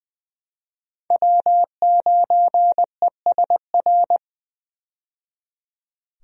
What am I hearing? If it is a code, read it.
Morse code, heard as a keyed tone, about 20 wpm, 700 Hz. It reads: W9ESR
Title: USA Call Signs - 5 Characters - 20wpm